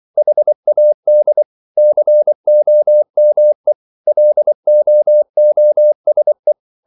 Morse code, heard as a keyed tone, about 24 wpm, 600 Hz. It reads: HAD COME LOOSE